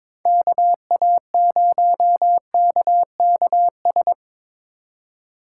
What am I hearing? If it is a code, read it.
Morse code, heard as a keyed tone, about 22 wpm, 700 Hz. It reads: KA0KKS
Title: USA Call Signs - 6 Characters - 2x Character Spacing - 22wpm